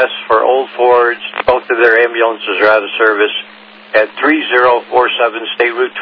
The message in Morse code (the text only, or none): none